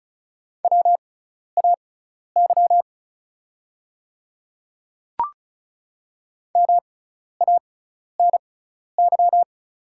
Morse code, heard as a keyed tone, about 36 wpm, spaced as for 12 wpm, 690 Hz. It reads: WAY MANY
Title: 6x Character Spacing - Top 100 Words - 35wpm